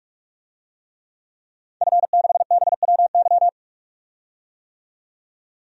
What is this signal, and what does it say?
Morse code, heard as a keyed tone, about 43 wpm, 700 Hz. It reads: F6BWY
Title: DXCC-100 Call Signs - 5 Characters - 45wpm